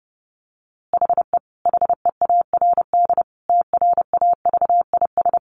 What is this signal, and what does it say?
Morse code, heard as a keyed tone, about 30 wpm, 700 Hz. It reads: HE HEARD TRAVIS
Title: A Sound Of Thunder 30wpm